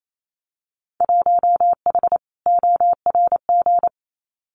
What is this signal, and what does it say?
Morse code, heard as a keyed tone, about 28 wpm, 700 Hz. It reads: 1H ORG
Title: Winter Field Day Exchanges 28wpm